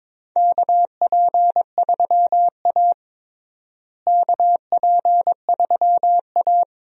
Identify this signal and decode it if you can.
Morse code, heard as a keyed tone, about 22 wpm, 700 Hz. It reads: KP3A KP3A
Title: USA Call Signs - 4 Characters - Repeated 2x - 22wpm